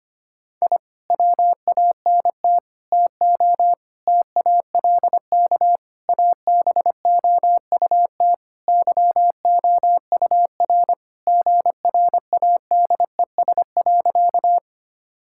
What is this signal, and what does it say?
Morse code, heard as a keyed tone, about 25 wpm, 700 Hz. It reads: I WANT TO TALK ABOUT YOUR GRADES.
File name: practice / 25wpm / Sentences with 4-Word Phrases - 25wpm